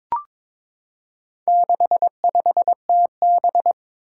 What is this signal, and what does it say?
Morse code, heard as a keyed tone, about 21 wpm, 700 Hz. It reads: 65TB
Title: Four Challenging Letter-Number Combinations 22wpm